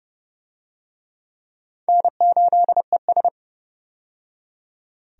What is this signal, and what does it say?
Morse code, heard as a keyed tone, about 30 wpm, 700 Hz. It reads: N8ES
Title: USA Call Signs - 4 Characters - 30wpm